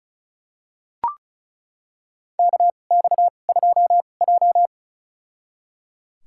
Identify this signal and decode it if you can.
Morse code, heard as a keyed tone, about 34 wpm, spaced as for 25 wpm, 700 Hz. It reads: KX2J
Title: USA Call Signs - 2x Character Spacing - 35wpm